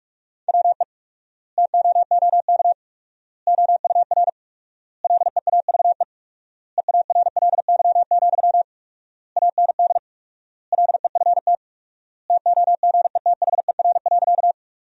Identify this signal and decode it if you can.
Morse code, heard as a keyed tone, about 46 wpm, 700 Hz. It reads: WE TOOK OUR LEAVE EARLY, AND LEFT TOGETHER.
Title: 4x Word-Spacing - Sentences from Top 400 Words - 45wpm